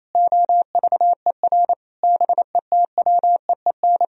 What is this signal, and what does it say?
Morse code, heard as a keyed tone, about 27 wpm, 700 Hz. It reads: OVER BETWEEN
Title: Sentences from Top 1000 Words - Review Part 1 - 28wpm